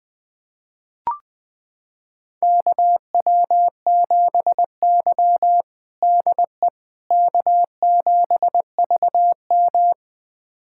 Morse code, heard as a keyed tone, about 20 wpm, 700 Hz. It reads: KW7Y DE K7VM
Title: USA Callsign DE Callsign - 4 Characters - 20wpm